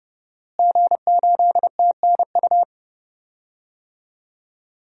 Morse code, heard as a keyed tone, about 30 wpm, 700 Hz. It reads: G8TNU
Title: DXCC-100 Call Signs - 5 Characters - Repeated 3x - 30wpm